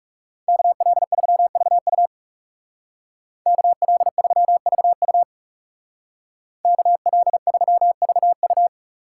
Morse code, heard as a keyed tone, about 36 wpm, 700 Hz. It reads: KL3VU KL3VU KL3VU
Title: Speed Racing - USA 2x2 Call Signs 30wpm